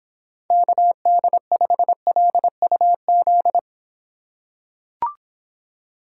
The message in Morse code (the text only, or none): KD5LUZ